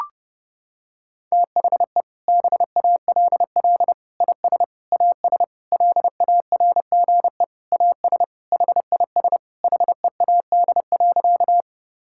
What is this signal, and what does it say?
Morse code, heard as a keyed tone, about 30 wpm, 700 Hz. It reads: THE BALL IS AS LARGE AS HIS HEAD.
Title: Sentences from Top 1000 Words - Review Part 6 - 30wpm